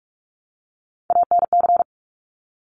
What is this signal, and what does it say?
Morse code, heard as a keyed tone, about 46 wpm, 700 Hz. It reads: ANC
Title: Top Three Letter Combinations in Words 45wpm